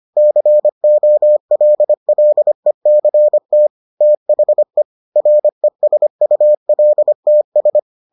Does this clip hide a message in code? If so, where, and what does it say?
Yes: Morse code from 0.2 to 7.8 s, heard as a keyed tone, about 25 wpm, 600 Hz. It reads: COLLECT THE RESULTS